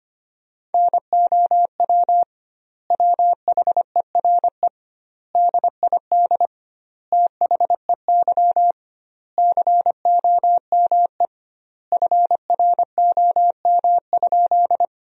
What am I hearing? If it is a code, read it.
Morse code, heard as a keyed tone, about 25 wpm, 700 Hz. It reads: NOW WHERE DID THEY COME FROM?
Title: 2x Word-Spacing - Sentences from Top 200 Words - 25wpm